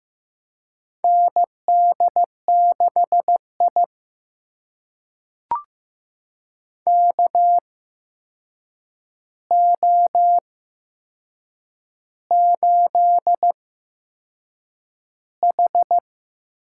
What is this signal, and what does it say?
Morse code, heard as a keyed tone, about 15 wpm, spaced as for 6 wpm, 700 Hz. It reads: ND6I KO8H
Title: USA Call Signs - 4 Characters - 8x Character Spacing - 15wpm